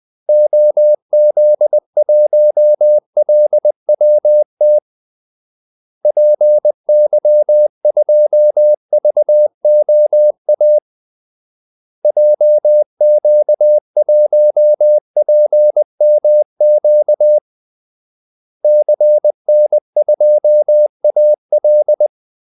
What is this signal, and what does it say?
Morse code, heard as a keyed tone, about 20 wpm, 600 Hz. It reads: OZ1LWT PY2VOA JQ1PMQ CN2AL
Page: Kaldesignaler 20 wpm